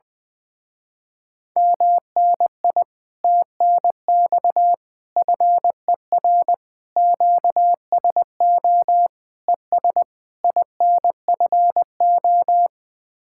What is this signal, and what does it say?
Morse code, heard as a keyed tone, about 20 wpm, 700 Hz. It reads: MNI TNX FER QSO ES INFO
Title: Speed-Racing Common QSO Phrases Encore - 20wpm